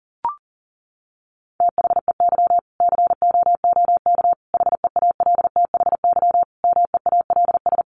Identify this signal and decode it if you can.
Morse code, heard as a keyed tone, about 40 wpm, 700 Hz. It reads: THEY COOK HEALTHY MEALS
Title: Easy Phrases 40wpm